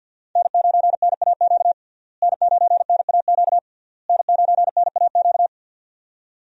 Morse code, heard as a keyed tone, about 52 wpm, 700 Hz. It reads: N9NAQ N9NAQ N9NAQ
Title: USA Call Signs - 5 Characters - Repeated 3x - 50wpm